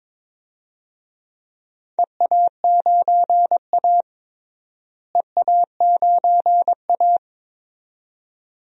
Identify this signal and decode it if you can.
Morse code, heard as a keyed tone, about 22 wpm, 700 Hz. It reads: EA9A EA9A
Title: DXCC-100 Call Signs - 4 Characters - Repeated 2x - 22wpm